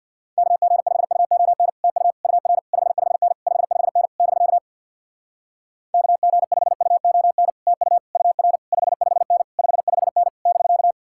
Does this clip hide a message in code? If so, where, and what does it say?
Yes: Morse code from 0.4 to 10.9 s, heard as a keyed tone, about 64 wpm, 700 Hz. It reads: KG5UON TU UR 55N 55N <BK> KG5UON TU UR 55N 55N <BK>